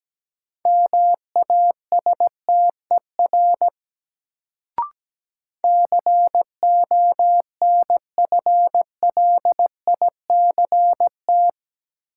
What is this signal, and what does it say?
Morse code, heard as a keyed tone, about 17 wpm, 700 Hz. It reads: MASTER CONFLICT
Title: Sentences from Top 1300 Words - New Word Drill - 17wpm